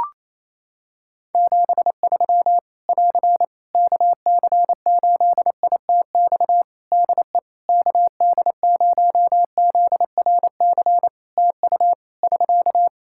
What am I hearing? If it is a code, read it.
Morse code, heard as a keyed tone, about 28 wpm, 700 Hz. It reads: 73 <AR> KC8ITX DE KD0ZRC TU <SK>
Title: Practice QSO - Exchange 4 - Ending - 28wpm